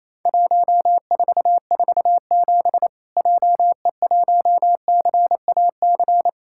Morse code, heard as a keyed tone, about 28 wpm, 700 Hz. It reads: 1447 JE1CAC
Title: Worked All Europe DX Contest - QTC Exchange - 28wpm